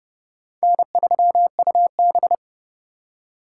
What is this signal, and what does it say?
Morse code, heard as a keyed tone, about 29 wpm, 700 Hz. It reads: N3UB
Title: USA Call Signs - 4 Characters - 3x Character Spacing - 30wpm